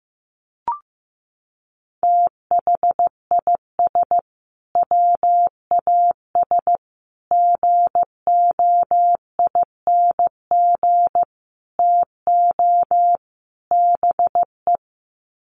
Morse code, heard as a keyed tone, about 15 wpm, 700 Hz. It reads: THIS WAS GOING TO BE
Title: Sets of 5 Words Encore2 15wpm